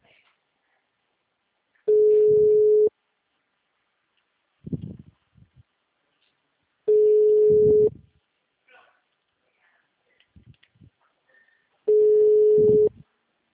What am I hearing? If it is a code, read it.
Ringback tone (Europe, 425 Hz), with speech.